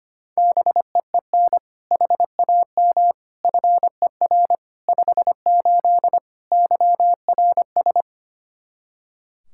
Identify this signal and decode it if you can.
Morse code, heard as a keyed tone, about 25 wpm, 700 Hz. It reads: BEEN HAM FER 58 YRS
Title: Practice QSO - Exchange 3 - More Info - 25wpm